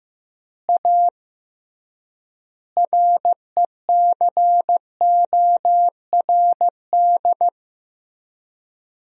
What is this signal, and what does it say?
Morse code, heard as a keyed tone, about 16 wpm, 700 Hz. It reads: A RECORD
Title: 3x Word-Spacing - Sentences from Top 500 Words - 15wpm